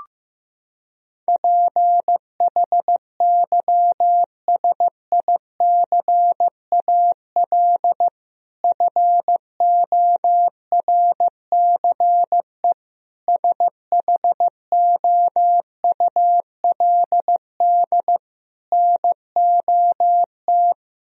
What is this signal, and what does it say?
Morse code, heard as a keyed tone, about 15 wpm, 700 Hz. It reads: PHYSICAL FORCE SHOULD NOT
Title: Sentences from Top 1000 Words - Review Part 4 - 15wpm